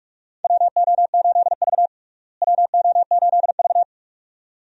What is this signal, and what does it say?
Morse code, heard as a keyed tone, about 41 wpm, 700 Hz. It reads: WO8V WO8V
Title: USA Call Signs - 4 Characters - Repeated 2x - 45wpm